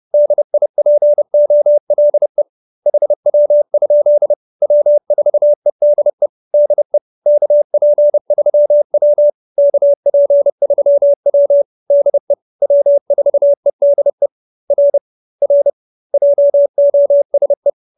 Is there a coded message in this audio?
Yes: Morse code from 0.1 to 17.7 s, heard as a keyed tone, about 30 wpm, 590 Hz. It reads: DIPOLE HW? W4EDE DE KP3W KP3W DE W4EDE R R JOSE